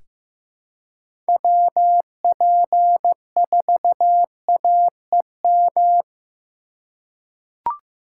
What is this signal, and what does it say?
Morse code, heard as a keyed tone, about 16 wpm, 700 Hz. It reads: WP4AEM
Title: USA Call Signs - 2x Character Spacing - 15wpm